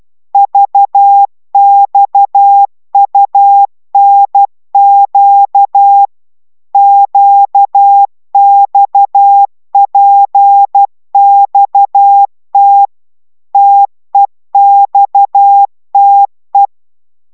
Morse code, heard as a keyed tone, about 12 wpm, 800 Hz. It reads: VXUNQ QXPXT TEXTE
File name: taviro tanulas / cw 69 16 10